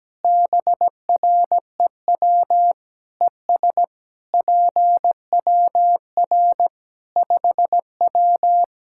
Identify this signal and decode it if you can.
Morse code, heard as a keyed tone, about 17 wpm, 700 Hz. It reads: BREW ES PWR 5W
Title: K1USN SST QSO Practice 17wpm